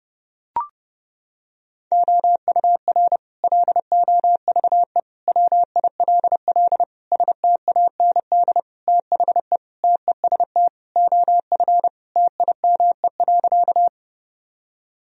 Morse code, heard as a keyed tone, about 30 wpm, 700 Hz. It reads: OUR LOVE WILL STAND THE TEST OF TIME.
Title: Sentences from Top 1000 Words - Review Part 3 - 30wpm